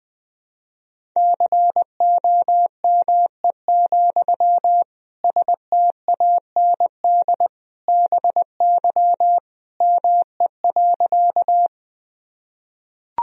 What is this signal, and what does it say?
Morse code, heard as a keyed tone, about 21 wpm, 700 Hz. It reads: COME, STAND BY ME.
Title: Sentences from Top 1000 Words - Review Part 1 - 20wpm